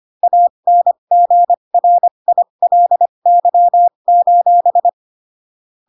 Morse code, heard as a keyed tone, about 25 wpm, spaced as for 22 wpm, 700 Hz. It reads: ANGRILY:
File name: Pinocchio / Pinocchio ch6 18 25 ik1hge